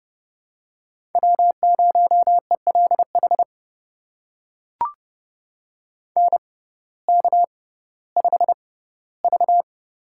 Morse code, heard as a keyed tone, about 30 wpm, spaced as for 16 wpm, 700 Hz. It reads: W0ELH NK5V
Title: USA Call Signs - 6x Character Spacing - 30wpm